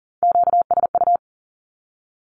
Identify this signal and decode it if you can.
Morse code, heard as a keyed tone, about 40 wpm, 700 Hz. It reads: QSU
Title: Q-Codes Repeated 3x 40wpm